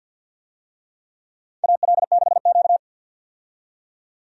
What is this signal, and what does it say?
Morse code, heard as a keyed tone, about 49 wpm, 700 Hz. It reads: AL6Q
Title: USA Call Signs - 4 Characters - 50wpm